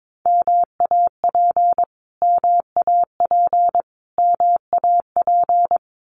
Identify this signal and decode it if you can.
Morse code, heard as a keyed tone, about 22 wpm, 700 Hz. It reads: MAP MAP MAP
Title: Top 500 Words Repeated 22wpm